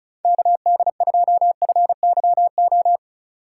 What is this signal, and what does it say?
Morse code, heard as a keyed tone, about 36 wpm, 700 Hz. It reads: KD2FYO
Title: USA Call Signs - 6 Characters - 3x Character Spacing - 35wpm